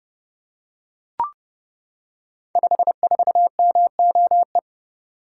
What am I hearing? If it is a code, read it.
Morse code, heard as a keyed tone, about 30 wpm, 700 Hz. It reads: 54MOE
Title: Code Groups - 5 Letter:Numbers - 30wpm